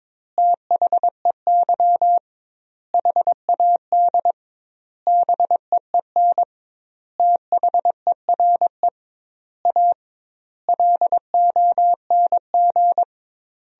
Morse code, heard as a keyed tone, about 22 wpm, 700 Hz. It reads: THEY HAD BEEN THERE A LONG
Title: 2x Word-Spacing - Sentences from Top 400 Words - 22wpm